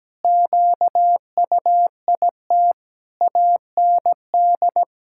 Morse code, heard as a keyed tone, about 17 wpm, 700 Hz. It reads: QUIT AND
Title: Pangram Challenge v2 - 17wpm